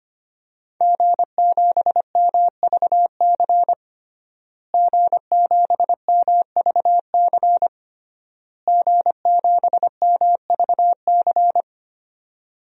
Morse code, heard as a keyed tone, about 25 wpm, 700 Hz. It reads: G7MVC G7MVC G7MVC
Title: DXCC-100 Call Signs - 5 Characters - Repeated 3x - 25wpm